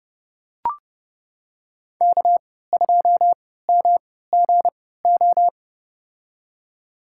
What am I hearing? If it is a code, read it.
Morse code, heard as a keyed tone, about 30 wpm, spaced as for 17 wpm, 700 Hz. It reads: K2MGO
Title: USA Call Signs - 5 Characters - 3x Character Spacing - 30wpm